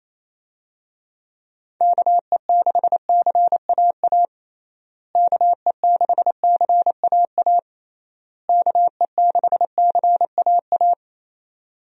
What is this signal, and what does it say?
Morse code, heard as a keyed tone, about 27 wpm, 700 Hz. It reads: KE6CAA KE6CAA KE6CAA
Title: USA Call Signs - 6 Characters - Repeated 3x - 28wpm